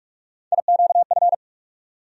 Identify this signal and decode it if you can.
Morse code, heard as a keyed tone, about 45 wpm, 700 Hz. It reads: IQF
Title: USA Call Sign Postfixes - 3 Letters 45wpm